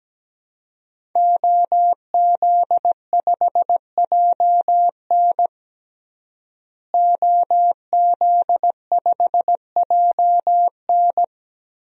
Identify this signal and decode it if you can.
Morse code, heard as a keyed tone, about 17 wpm, 700 Hz. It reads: OZ5JN OZ5JN
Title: DXCC-100 Call Signs - 5 Characters - Repeated 3x - 17wpm